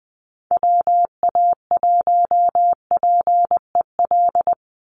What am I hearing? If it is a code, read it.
Morse code, heard as a keyed tone, about 20 wpm, 700 Hz. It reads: WA1PEL
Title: USA Call Signs 20wpm